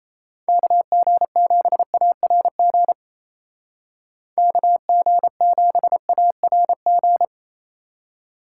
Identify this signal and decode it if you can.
Morse code, heard as a keyed tone, about 34 wpm, 700 Hz. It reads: KG7ARG KG7ARG
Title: Speed Racing - USA 2x3 Call Signs 28wpm